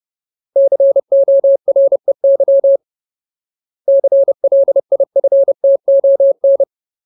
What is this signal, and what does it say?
Morse code, heard as a keyed tone, about 30 wpm, 550 Hz. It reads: COREY CLIFTON
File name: Ultimate Word List / 30WPM Ultimate